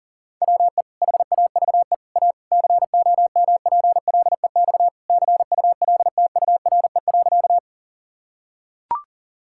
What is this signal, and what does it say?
Morse code, heard as a keyed tone, about 37 wpm, 700 Hz. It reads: WE HAVE A COMPLEX CULTURE.
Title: Sentences from Top 1000 Words - Review Part 6 - 40wpm